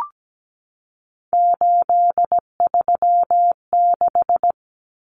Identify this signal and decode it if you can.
Morse code, heard as a keyed tone, about 17 wpm, 700 Hz. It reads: 836
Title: Common Cut Numbers - 3-Digit 17wpm